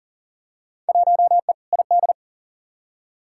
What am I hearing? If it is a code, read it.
Morse code, heard as a keyed tone, about 38 wpm, 700 Hz. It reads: 1E ID
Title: Field Day Exchanges - 40wpm